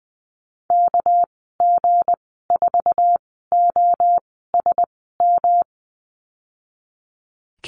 Morse code, heard as a keyed tone, about 20 wpm, spaced as for 15 wpm, 700 Hz. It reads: KG4OSM
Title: 207 Lesson 33g - Letter j - Call Signs 20wpm